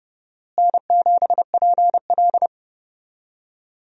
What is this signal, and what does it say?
Morse code, heard as a keyed tone, about 30 wpm, 700 Hz. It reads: N7PL
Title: USA Call Signs - 4 Characters - Repeated 3x - 30wpm